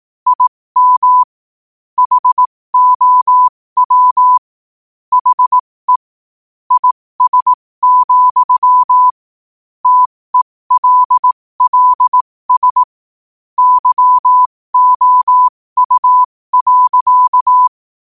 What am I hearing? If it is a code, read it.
Morse code, heard as a keyed tone, about 18 wpm, spaced as for 15 wpm, 1000 Hz. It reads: IM HOW HE IS, TELLS YOU.
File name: qotdfiles / 15wpm